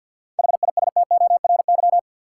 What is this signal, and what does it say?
Morse code, heard as a keyed tone, about 53 wpm, 700 Hz. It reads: HISTORY
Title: Sets of 3 Words - New Word Drill - 50wpm